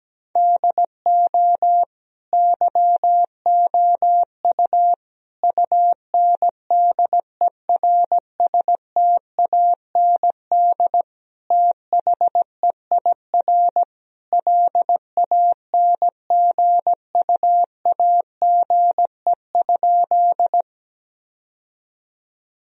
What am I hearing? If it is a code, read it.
Morse code, heard as a keyed tone, about 17 wpm, 700 Hz. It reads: DO YOU UNDERSTAND THEIR LANGUAGE?
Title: Sentences from Top 1000 Words - Review Part 3 - 17wpm